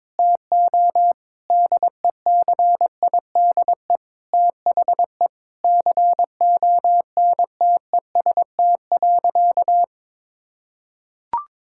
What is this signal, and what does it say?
Morse code, heard as a keyed tone, about 22 wpm, 700 Hz. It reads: TO DECIDE THE CONTEST.
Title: Sentences from Top 1600 Words 22wpm